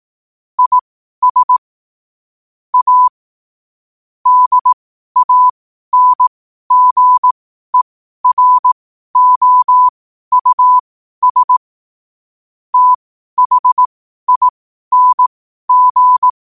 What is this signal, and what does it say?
Morse code, heard as a keyed tone, about 18 wpm, spaced as for 12 wpm, 1000 Hz. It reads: IS A DANGEROUS THING
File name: qotdfiles / 12wpm